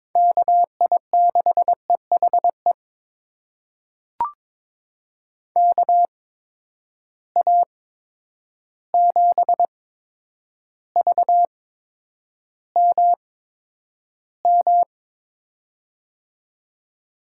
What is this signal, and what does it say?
Morse code, heard as a keyed tone, about 22 wpm, spaced as for 9 wpm, 700 Hz. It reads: KI6EHE KA7VMM
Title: USA Call Signs - 6 Characters - 8x Character Spacing - 22wpm